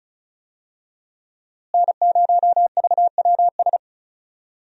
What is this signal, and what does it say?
Morse code, heard as a keyed tone, about 34 wpm, 700 Hz. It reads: N0VWS
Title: USA Call Signs - 5 Characters - Repeated 3x - 35wpm